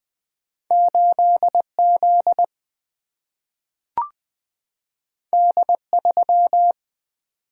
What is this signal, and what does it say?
Morse code, heard as a keyed tone, about 20 wpm, 700 Hz. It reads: 8 Z D 3
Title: Single-Letter Single-Number 20wpm